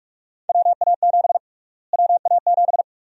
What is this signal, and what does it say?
Morse code, heard as a keyed tone, about 45 wpm, 700 Hz. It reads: WA7 WA7
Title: USA Call Sign Prefixes - Repeated 2x - 45wpm